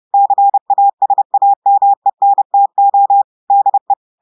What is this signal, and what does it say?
Morse code, heard as a keyed tone, about 30 wpm, 800 Hz. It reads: CASAMENTO DE
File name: audios / portugais / portugais texte en clair 1 vitesse 30 mots minutes